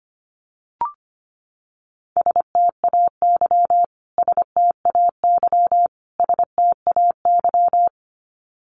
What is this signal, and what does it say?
Morse code, heard as a keyed tone, about 25 wpm, 700 Hz. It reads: STAY STAY STAY
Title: Top 500 Words Repeated 25wpm